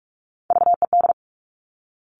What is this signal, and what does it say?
Morse code, heard as a keyed tone, about 46 wpm, 700 Hz. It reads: VED
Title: Top Three Letter Combinations in Words 45wpm